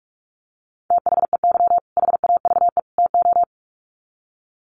Morse code, heard as a keyed tone, about 44 wpm, 700 Hz. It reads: THEY HAVE TO
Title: Sets of 3 Words 45wpm